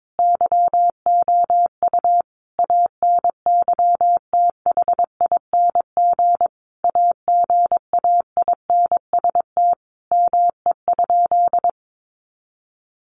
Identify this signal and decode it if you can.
Morse code, heard as a keyed tone, about 22 wpm, 700 Hz. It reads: YOU ANYTHING AGAINST ME?
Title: Sentences from Top 1000 Words 22wpm